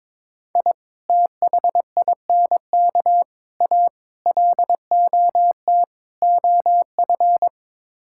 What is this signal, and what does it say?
Morse code, heard as a keyed tone, about 22 wpm, 700 Hz. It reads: I THINK A LOT OF
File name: practice / 22wpm / Build A Phrase - 5-Word Phrases - 22wpm